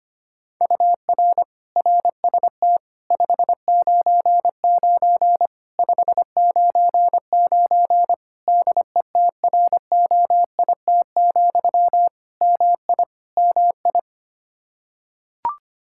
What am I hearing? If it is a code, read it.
Morse code, heard as a keyed tone, about 25 wpm, 700 Hz. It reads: UR RST 599 599 DETROIT, MI MI
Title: Practice QSO - Exchange 1 - Beginning - 25wpm